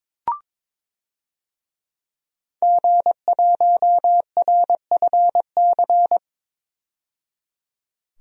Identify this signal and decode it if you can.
Morse code, heard as a keyed tone, about 22 wpm, 700 Hz. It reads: G1RFC
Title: DXCC-100 Call Signs - 5 Characters - 22wpm